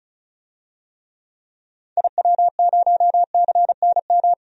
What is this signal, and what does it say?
Morse code, heard as a keyed tone, about 34 wpm, 700 Hz. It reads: IW0CNM
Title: DXCC-100 Call Signs - 6 Characters - 35wpm